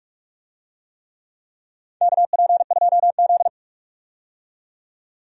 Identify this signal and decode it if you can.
Morse code, heard as a keyed tone, about 46 wpm, 690 Hz. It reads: KP2Z
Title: DXCC-100 Call Signs - 4 Characters - 45wpm